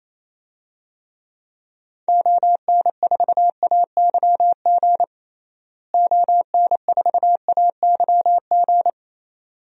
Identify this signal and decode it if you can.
Morse code, heard as a keyed tone, about 28 wpm, 700 Hz. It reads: ON4AYG ON4AYG
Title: DXCC-100 Call Signs - 6 Characters - Repeated 2x - 28wpm